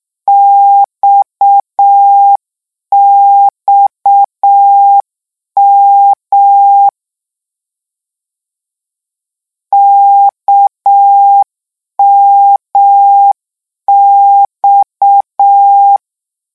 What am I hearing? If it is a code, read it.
Morse code, heard as a keyed tone, about 6 wpm, 800 Hz. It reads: XXM KMX
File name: lecon 1-vitesse 6